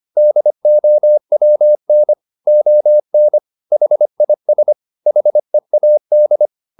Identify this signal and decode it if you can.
Morse code, heard as a keyed tone, about 25 wpm, 600 Hz. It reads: DOWN ON HIS HEAD
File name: the old man and the sea part 1 25wpm